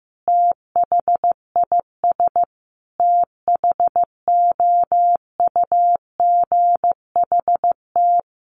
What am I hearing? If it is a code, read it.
Morse code, heard as a keyed tone, about 15 wpm, 700 Hz. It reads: THIS THOUGHT
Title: Sentences from Top 900 Words 15wpm